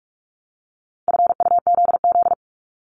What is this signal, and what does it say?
Morse code, heard as a keyed tone, about 46 wpm, 700 Hz. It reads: FUZZ
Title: Multi-letter Phonemes 45wpm